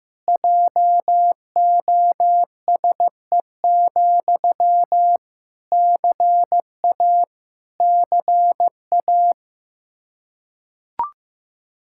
Morse code, heard as a keyed tone, about 16 wpm, 700 Hz. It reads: JOSE, CA CA
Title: Practice QSO - Exchange 1 - Beginning - 15wpm